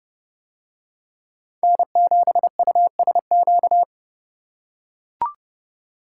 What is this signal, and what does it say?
Morse code, heard as a keyed tone, about 29 wpm, 700 Hz. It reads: N7USQ